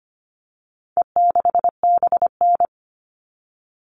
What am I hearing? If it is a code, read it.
Morse code, heard as a keyed tone, about 25 wpm, 700 Hz. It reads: E6BN
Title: All Cut Numbers - 4-Digit 25wpm